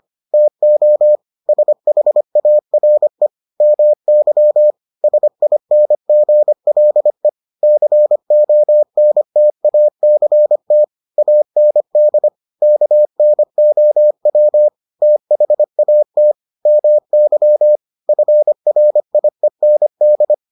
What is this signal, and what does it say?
Morse code, heard as a keyed tone, about 25 wpm, 600 Hz. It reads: TO SHARE MY SINGLE CONTACT AND KNOW THAT MY FRIEND